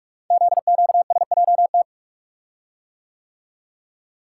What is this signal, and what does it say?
Morse code, heard as a keyed tone, about 44 wpm, 700 Hz. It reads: ZQSJT
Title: Code Groups - 5 Letters - 45wpm